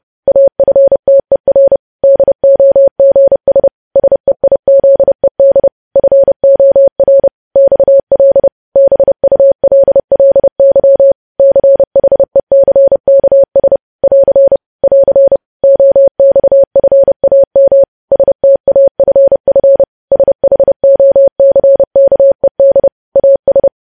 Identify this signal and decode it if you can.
Morse code, heard as a keyed tone, about 30 wpm, 550 Hz. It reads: AFTER DOGS SEIZED FOR XL BULLY CHECKS <AR> <AR> OXFAM STAFF SHOCKED AS
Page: news headlines in morse code at 30 wpm